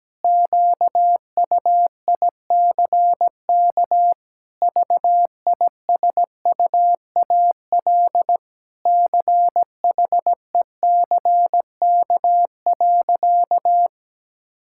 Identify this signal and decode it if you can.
Morse code, heard as a keyed tone, about 17 wpm, 700 Hz. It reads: QUICK VISUAL CHECK.
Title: Sentences from Top 1000 Words - Review Part 6 - 17wpm